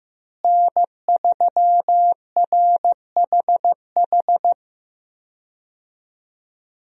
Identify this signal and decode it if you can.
Morse code, heard as a keyed tone, about 15 wpm, 700 Hz. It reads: N3RHH
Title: USA Call Signs - 5 Characters - Repeated 2x - 15wpm